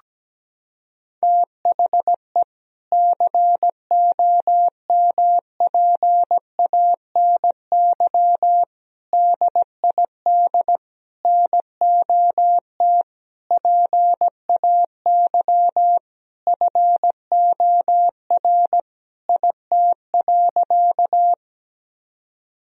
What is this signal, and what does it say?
Morse code, heard as a keyed tone, about 17 wpm, 700 Hz. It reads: THE COMPANY DID NOT PAY FOR IT.
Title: Sentences from Top 1000 Words - Review Part 3 - 17wpm